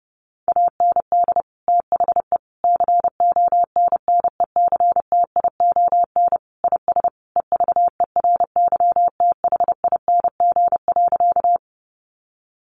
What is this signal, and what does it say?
Morse code, heard as a keyed tone, about 30 wpm, 700 Hz. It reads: AND THE CONNECTION IS EVERYTHING.
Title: Sentences from Top 1300 Words 30wpm